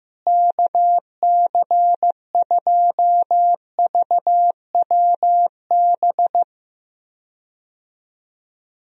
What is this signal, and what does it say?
Morse code, heard as a keyed tone, about 15 wpm, 700 Hz. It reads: KC2VWB
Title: USA Call Signs - 6 Characters - Repeated 3x - 15wpm